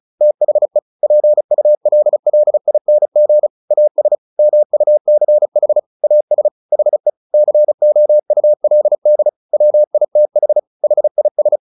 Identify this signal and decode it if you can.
Morse code, heard as a keyed tone, about 34 wpm, 600 Hz. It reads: THE PULLING AS MUCH AS HE COULD WITH HIS
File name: the old man and the sea 2 35wpm